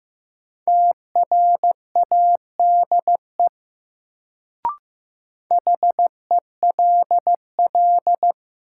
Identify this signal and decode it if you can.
Morse code, heard as a keyed tone, about 15 wpm, 700 Hz. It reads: TRADE HELL
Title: Sets of 5 Words Encore - New Word Drill - 15wpm